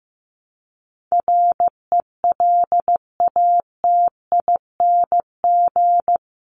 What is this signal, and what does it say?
Morse code, heard as a keyed tone, about 15 wpm, 700 Hz. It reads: RELATING
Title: ING Suffix - Top 100 Words with Base Words 15wpm